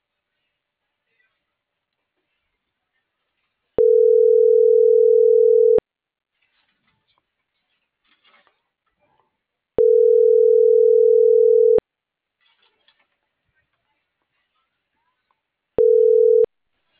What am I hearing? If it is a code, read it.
Ringback tone (North America), 440 + 480 Hz.